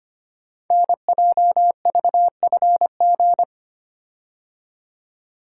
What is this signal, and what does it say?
Morse code, heard as a keyed tone, about 25 wpm, 700 Hz. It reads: NJVFG
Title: Code Groups - 5 Letters - 25wpm